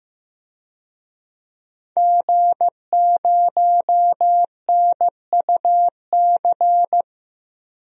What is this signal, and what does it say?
Morse code, heard as a keyed tone, about 15 wpm, 700 Hz. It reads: G0NUC
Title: DXCC-100 Call Signs - 5 Characters - Repeated 2x - 15wpm